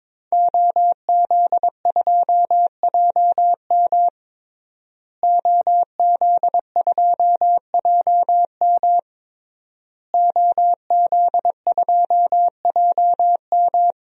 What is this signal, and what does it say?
Morse code, heard as a keyed tone, about 22 wpm, 700 Hz. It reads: OZ2JM OZ2JM OZ2JM
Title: DXCC-100 Call Signs - 5 Characters - Repeated 3x - 22wpm